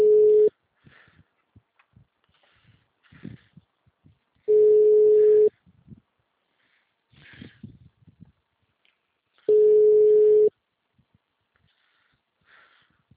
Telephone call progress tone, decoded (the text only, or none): ringback tone